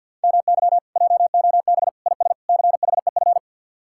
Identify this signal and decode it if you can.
Morse code, heard as a keyed tone, about 65 wpm, spaced as for 57 wpm, 700 Hz. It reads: MY JOB IS CHEF
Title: Speed-Racing Common QSO Phrases Encore - 50wpm